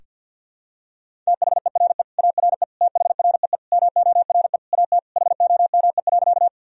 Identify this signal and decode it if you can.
Morse code, heard as a keyed tone, about 47 wpm, 700 Hz. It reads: THERE ARE THREE MORE AT HOME.
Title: Sentences from Top 1000 Words - Review Part 1 - 50wpm